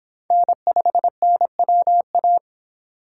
Morse code, heard as a keyed tone, about 24 wpm, 700 Hz. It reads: N5NWA